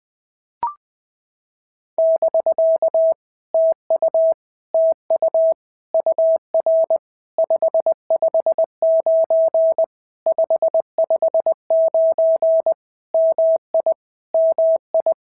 Morse code, heard as a keyed tone, about 20 wpm, 650 Hz. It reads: <BK> TU TU UR 559 559 MI MI
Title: POTA QSO Practice - 20wpm